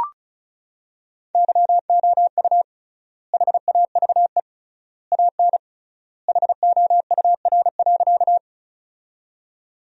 Morse code, heard as a keyed tone, about 36 wpm, 700 Hz. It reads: YOU HAVE AN HOUR.
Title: 3x Word-Spacing - Sentences from Top 400 Words - 35wpm